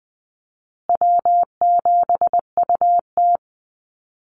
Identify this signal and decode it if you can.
Morse code, heard as a keyed tone, about 20 wpm, 700 Hz. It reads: W7UT
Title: Easy Call Signs Repeated 2x - 20wpm